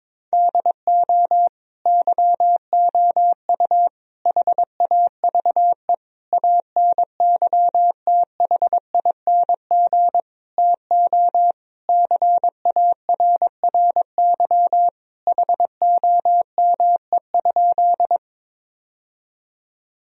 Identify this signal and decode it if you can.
Morse code, heard as a keyed tone, about 22 wpm, 700 Hz. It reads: DO YOU HAVE ANYTHING TO CARRY HOME?
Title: Sentences from Top 1000 Words - Review Part 1 - 22wpm